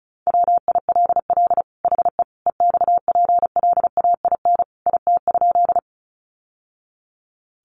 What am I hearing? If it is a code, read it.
Morse code, heard as a keyed tone, about 34 wpm, 700 Hz. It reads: WILL HE EXPLAIN IT?
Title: Sentences from Top 1100 Words 35wpm